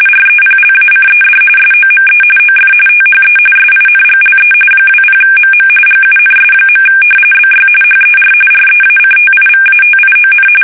RTTY, about 45.45 baud. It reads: REPLACED SOME OF THE SINGLE 20MM GUNS. BY HER REACTIVATION
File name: fsk850 / battleshipnj